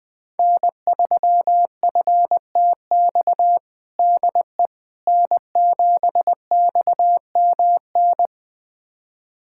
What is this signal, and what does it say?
Morse code, heard as a keyed tone, about 20 wpm, 700 Hz. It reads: N3FTX DE N7XMN
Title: USA Callsign DE Callsign - 5 Characters - 20wpm